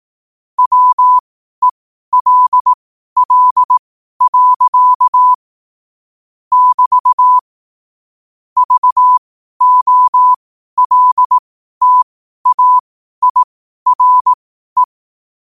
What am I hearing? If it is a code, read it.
Morse code, heard as a keyed tone, about 18 wpm, spaced as for 12 wpm, 1000 Hz. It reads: WELL. <BT> VOLTAIRE
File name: qotdfiles / 12wpm